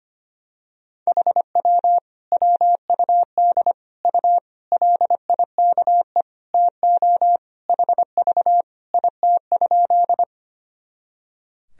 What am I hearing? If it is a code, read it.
Morse code, heard as a keyed tone, about 25 wpm, 700 Hz. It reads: HW WUD U LIKE TO HV IT?
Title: Sentences from Top 100 Words and Abbreviations 25wpm